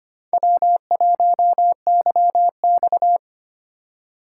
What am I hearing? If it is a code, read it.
Morse code, heard as a keyed tone, about 32 wpm, 700 Hz. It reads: W1YX